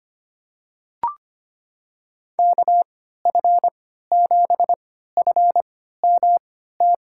Morse code, heard as a keyed tone, about 25 wpm, spaced as for 14 wpm, 700 Hz. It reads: KF7FMT
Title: USA Call Signs - 6 Characters - 3x Character Spacing - 25wpm